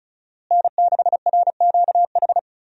Morse code, heard as a keyed tone, about 36 wpm, 700 Hz. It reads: N6RQH
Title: USA Call Signs - 5 Characters - Repeated 3x - 35wpm